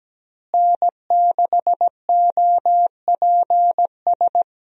Morse code, heard as a keyed tone, about 17 wpm, 700 Hz. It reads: N6OPS
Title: USA Call Signs - 5 Characters - Repeated 2x - 17wpm